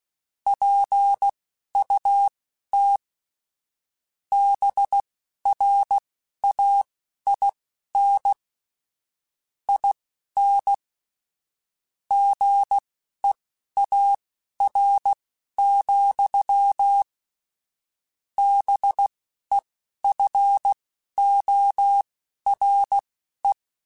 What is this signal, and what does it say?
Morse code, heard as a keyed tone, about 16 wpm, spaced as for 12 wpm, 800 Hz. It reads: PUT BRAIN IN GEAR, BEFORE
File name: Chuckles / Chuckles-mp3s / BrainMouth @10